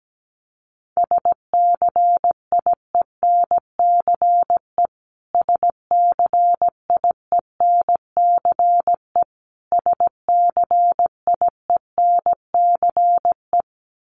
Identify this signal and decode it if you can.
Morse code, heard as a keyed tone, about 17 wpm, 700 Hz. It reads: SCIENCE SCIENCE SCIENCE
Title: Top 500 Words Repeated 17wpm